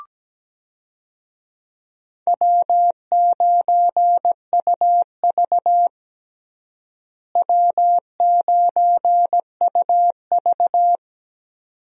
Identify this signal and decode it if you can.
Morse code, heard as a keyed tone, about 17 wpm, 700 Hz. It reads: W9UV W9UV
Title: USA Call Signs - 4 Characters - Repeated 2x - 17wpm